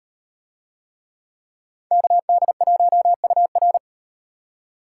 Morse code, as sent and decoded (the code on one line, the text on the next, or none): -.- -.. .---- ..- .-.
KD1UR